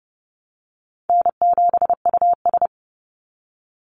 Morse code, heard as a keyed tone, about 30 wpm, 700 Hz. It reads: N7US
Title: Call Signs Repeated 2x - 30wpm